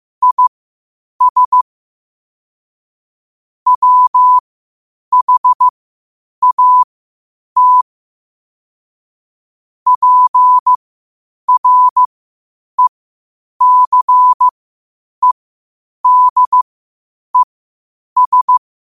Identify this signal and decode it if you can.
Morse code, heard as a keyed tone, about 15 wpm, spaced as for 8 wpm, 1000 Hz. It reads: IS WHAT PRECEDES
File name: qotdfiles / 8wpm